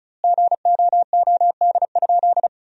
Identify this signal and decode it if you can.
Morse code, heard as a keyed tone, about 34 wpm, 700 Hz. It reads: GOOD?
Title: 2x Word-Spacing - Sentences from Top 200 Words - 35wpm